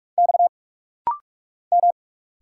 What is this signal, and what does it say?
Morse code, heard as a keyed tone, about 46 wpm, 700 Hz. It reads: X M
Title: Single Letter-Number Alt - Rapid-Fire 45wpm